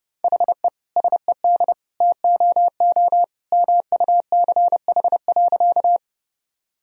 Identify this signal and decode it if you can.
Morse code, heard as a keyed tone, about 30 wpm, 700 Hz. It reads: HE SED TOO MUCH.
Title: Sentences from Top 200 Words and Abbreviations 30wpm